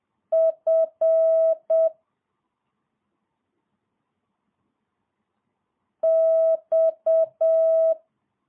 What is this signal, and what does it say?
Morse code, heard as a keyed tone, about 7 wpm, 650 Hz. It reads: F X
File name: jh8cbh / 2022cw